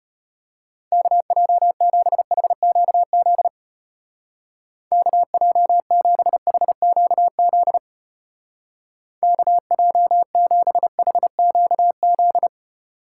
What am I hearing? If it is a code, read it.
Morse code, heard as a keyed tone, about 31 wpm, 700 Hz. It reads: KJ7HQZ KJ7HQZ KJ7HQZ
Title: Speed Racing - USA 2x3 Call Signs 25wpm